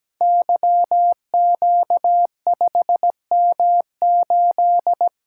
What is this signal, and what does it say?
Morse code, heard as a keyed tone, about 17 wpm, 700 Hz. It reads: YQ5M8
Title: Code Groups - 5 Letter:Numbers - 17wpm